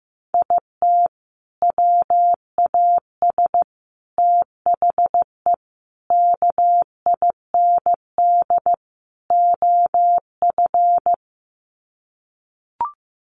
Morse code, heard as a keyed tone, about 15 wpm, 700 Hz. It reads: IT WAS THE KIND OF
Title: Sets of 5 Words Encore2 15wpm